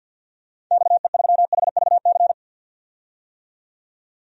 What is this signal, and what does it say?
Morse code, heard as a keyed tone, about 51 wpm, 700 Hz. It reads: XE3HVC
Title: DXCC-100 Call Signs - 6 Characters - 50wpm